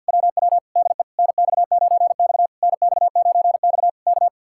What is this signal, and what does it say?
Morse code, heard as a keyed tone, about 65 wpm, 700 Hz. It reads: WW DE NX9X NX9X K